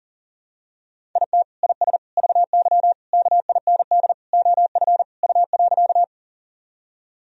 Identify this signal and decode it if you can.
Morse code, heard as a keyed tone, about 40 wpm, 700 Hz. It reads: IT IS VY KIND OF U.
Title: Sentences from Top 200 Words and Abbreviations 40wpm